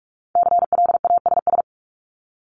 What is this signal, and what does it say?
Morse code, heard as a keyed tone, about 46 wpm, 700 Hz. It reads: CLASS
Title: Multi-letter Phonemes Repeated 3x 45wpm